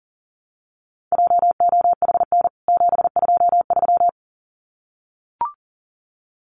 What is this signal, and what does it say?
Morse code, heard as a keyed tone, about 40 wpm, 700 Hz. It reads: JOHN 723
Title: CWT Exchange 40wpm